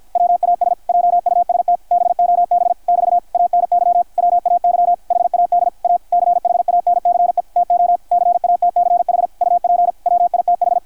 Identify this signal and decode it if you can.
Morse code, heard as a keyed tone, about 50 wpm, 660 Hz, with radio noise. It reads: WAS JUST BOB <BT> ANY WAY HAD A CHANCE TO CATCH UP WITH
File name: archived / CRSnet